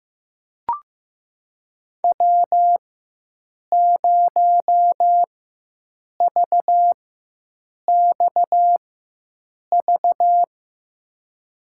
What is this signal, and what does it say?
Morse code, heard as a keyed tone, about 15 wpm, spaced as for 7 wpm, 700 Hz. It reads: W0VXV
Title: USA Call Signs - 4x Character Spacing - 15wpm